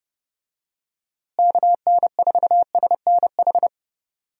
Morse code, heard as a keyed tone, about 31 wpm, 700 Hz. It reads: KN4SNH
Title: USA Call Signs - 6 Characters - Repeated 2x - 30wpm